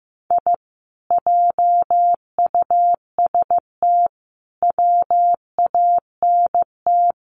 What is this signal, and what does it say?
Morse code, heard as a keyed tone, about 15 wpm, 700 Hz. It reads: I JUST WANT
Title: Sets of 5 Words Encore 15wpm